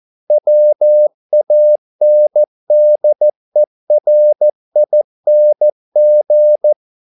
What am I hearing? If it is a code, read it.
Morse code, heard as a keyed tone, about 14 wpm, 600 Hz. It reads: WANDERING
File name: mp3 files / a dogs mistake - 14wpm